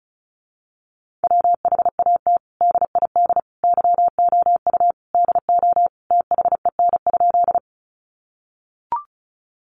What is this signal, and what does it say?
Morse code, heard as a keyed tone, about 34 wpm, 700 Hz. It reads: WHAT DID YOU DO THEN?
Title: Sentences from Top 100 Words 35wpm